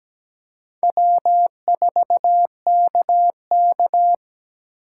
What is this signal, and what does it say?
Morse code, heard as a keyed tone, about 20 wpm, 700 Hz. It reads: W4KK
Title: Speed Racing - USA 1x2 Call Signs 17wpm